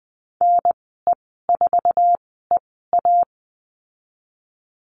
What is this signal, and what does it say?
Morse code, heard as a keyed tone, about 20 wpm, spaced as for 14 wpm, 700 Hz. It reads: NE4EA
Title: 062 Lesson 11g - Letter h - Call Signs 20wpm